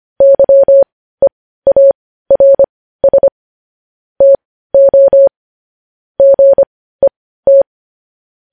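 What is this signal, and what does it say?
Morse code, heard as a keyed tone, about 25 wpm, spaced as for 15 wpm, 550 Hz. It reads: YEARS TO GET
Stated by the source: Morse news headlines sent at 15 words per minute.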